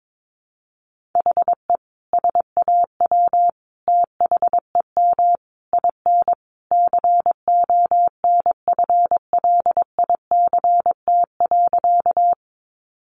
Morse code, heard as a keyed tone, about 22 wpm, 700 Hz. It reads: HE SAW THEM IN CONFLICT.
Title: Sentences from Top 1300 Words 22wpm